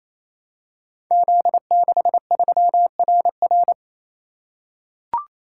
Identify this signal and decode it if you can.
Morse code, heard as a keyed tone, about 28 wpm, 700 Hz. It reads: Z63RR
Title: Code Groups - 5 Letter:Numbers - 28wpm